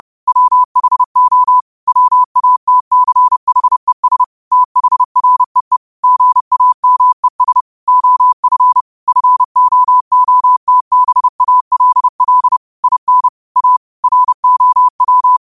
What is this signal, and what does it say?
Morse code, heard as a keyed tone, about 30 wpm, 1000 Hz. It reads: WHO WATCHES THREE GAMES OF FOOTBALL IN A ROW